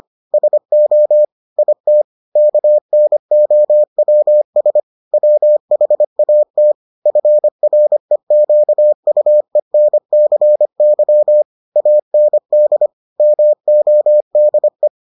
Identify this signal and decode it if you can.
Morse code, heard as a keyed tone, about 25 wpm, 600 Hz. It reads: SO IT KNOWS WHAT FREQUENCY AND MODE